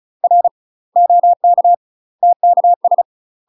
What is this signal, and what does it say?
Morse code, heard as a keyed tone, about 36 wpm, 700 Hz. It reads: R OK TKS
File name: UK MASTER QSOs TRAINING / 35WPM / UK MASTER QSOs 0039